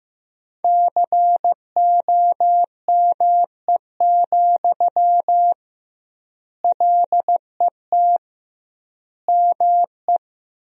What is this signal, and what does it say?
Morse code, heard as a keyed tone, about 15 wpm, 700 Hz. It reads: COME, LET ME
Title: 2x Word-Spacing - Sentences from Top 500 Words - 15wpm